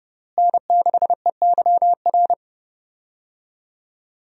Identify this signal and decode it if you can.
Morse code, heard as a keyed tone, about 30 wpm, 700 Hz. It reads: N6EYR
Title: USA Call Signs - 5 Characters - Repeated 3x - 30wpm